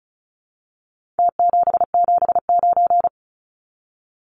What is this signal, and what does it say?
Morse code, heard as a keyed tone, about 34 wpm, 700 Hz. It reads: T779
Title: Numbers - Contest Style - 4-Digit 35wpm